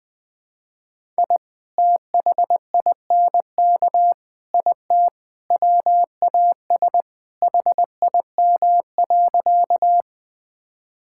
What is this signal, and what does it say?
Morse code, heard as a keyed tone, about 21 wpm, 700 Hz. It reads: I THINK IT WAS HIM.
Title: Sentences with 4-Word Phrases - Spoken Beginning - 20wpm